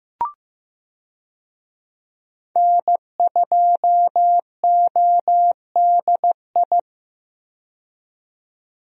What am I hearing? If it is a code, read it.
Morse code, heard as a keyed tone, about 16 wpm, 700 Hz. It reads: N2ODI
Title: USA Call Signs - 5 Characters - 15wpm